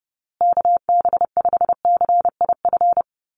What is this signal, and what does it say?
Morse code, heard as a keyed tone, about 30 wpm, 700 Hz. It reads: KB5CIF
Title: USA Call Signs Repeated 3x 30wpm